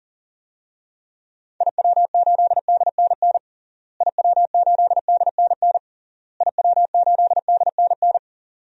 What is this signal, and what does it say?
Morse code, heard as a keyed tone, about 37 wpm, 700 Hz. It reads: IW8DNN IW8DNN IW8DNN
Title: DXCC-100 Call Signs - 6 Characters - Repeated 3x - 40wpm